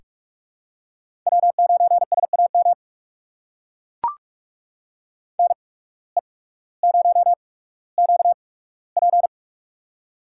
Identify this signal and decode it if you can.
Morse code, heard as a keyed tone, about 43 wpm, spaced as for 19 wpm, 700 Hz. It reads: W9SAM NE0QP
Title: USA Call Signs - 5 Characters - 8x Character Spacing - 45wpm